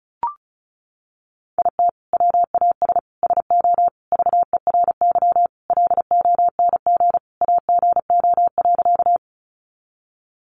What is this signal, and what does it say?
Morse code, heard as a keyed tone, about 34 wpm, 700 Hz. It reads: IT WAS SO VERY LONG AGO.
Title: Sentences from Top 500 Words 35wpm